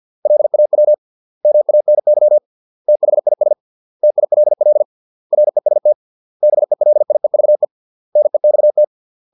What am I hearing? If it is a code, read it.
Morse code, heard as a keyed tone, about 50 wpm, 600 Hz. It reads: LAW MANY THIS TILL REST BELIEVE NEXT